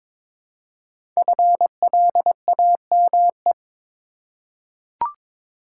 Morse code, heard as a keyed tone, about 22 wpm, 700 Hz. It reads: FLAME
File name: 5 Letter Words - First 3 Letters Given - 22wpm